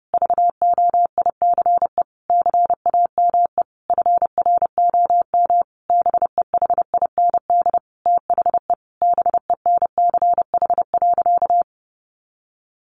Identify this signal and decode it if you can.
Morse code, heard as a keyed tone, about 30 wpm, 700 Hz. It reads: VOICE CAME FROM BEHIND THE BENCH.
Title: Sentences from Top 1400 Words 30wpm